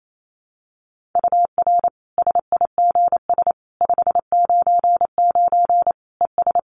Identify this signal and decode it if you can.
Morse code, heard as a keyed tone, about 28 wpm, 700 Hz. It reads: UR SIGS 599 ES
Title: Common QSO Phrases 28wpm